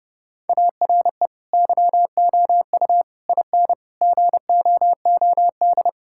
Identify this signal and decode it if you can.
Morse code, heard as a keyed tone, about 32 wpm, 700 Hz. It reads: ARE YOU IN GOOD
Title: Sentences from Top 1000 Words - Review Part 5 - 30wpm